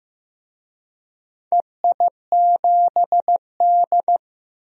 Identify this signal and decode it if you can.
Morse code, heard as a keyed tone, about 15 wpm, 700 Hz. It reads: EI7D
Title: DXCC-100 Call Signs - 4 Characters - Repeated 2x - 15wpm